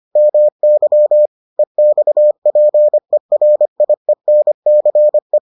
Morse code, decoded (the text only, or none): MY EXPERIENCE